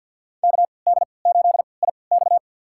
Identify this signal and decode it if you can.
Morse code, heard as a keyed tone, about 53 wpm, spaced as for 29 wpm, 700 Hz. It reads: KD8IX
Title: USA Call Signs - 5 Characters - 3x Character Spacing - 50wpm